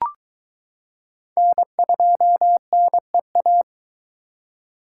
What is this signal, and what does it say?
Morse code, heard as a keyed tone, about 22 wpm, 700 Hz. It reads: N2NEA